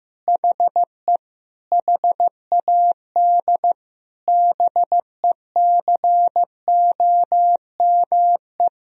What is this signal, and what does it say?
Morse code, heard as a keyed tone, about 15 wpm, 700 Hz. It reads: HE HAD BECOME
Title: Sentences from Top 1000 Words - Review Part 3 - 15wpm